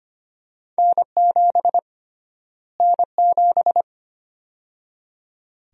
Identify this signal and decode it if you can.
Morse code, heard as a keyed tone, about 25 wpm, 700 Hz. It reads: N7 N7
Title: USA Call Sign Prefixes - Repeated 2x - 25wpm